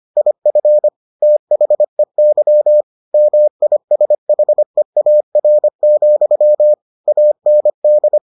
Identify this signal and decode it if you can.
Morse code, heard as a keyed tone, about 25 wpm, 600 Hz. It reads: IF THEY MISHEAR, AND